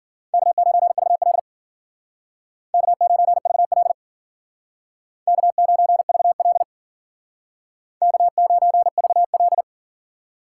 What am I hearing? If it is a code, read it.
Morse code, heard as a keyed tone, about 53 wpm, 700 Hz. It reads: K9VL K9VL K9VL K9VL